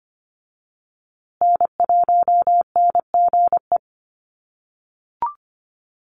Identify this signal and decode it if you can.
Morse code, heard as a keyed tone, about 25 wpm, 700 Hz. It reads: N1NGE
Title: USA Call Signs Repeated 3x 25wpm